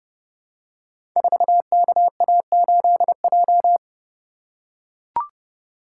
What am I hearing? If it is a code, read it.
Morse code, heard as a keyed tone, about 30 wpm, 700 Hz. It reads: 4KA8J
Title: Code Groups - 5 Letter:Numbers - 30wpm